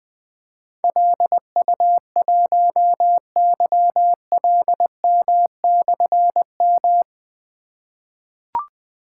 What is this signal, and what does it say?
Morse code, heard as a keyed tone, about 20 wpm, 700 Hz. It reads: LU1YLM/M
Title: Difficult Call Signs 20wpm